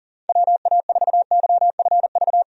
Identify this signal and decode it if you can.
Morse code, heard as a keyed tone, about 40 wpm, 700 Hz. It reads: WA4YFV
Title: USA Call Signs - 6 Characters - 40wpm